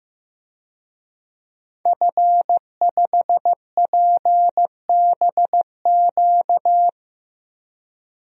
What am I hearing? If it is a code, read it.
Morse code, heard as a keyed tone, about 15 wpm, 700 Hz. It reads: F5PBQ
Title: DXCC-100 Call Signs - 5 Characters - 15wpm